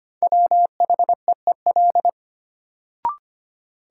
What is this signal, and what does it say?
Morse code, heard as a keyed tone, about 25 wpm, 700 Hz. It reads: WHEEL
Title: Sentences from Top 500 Words - New Word Drill - 25wpm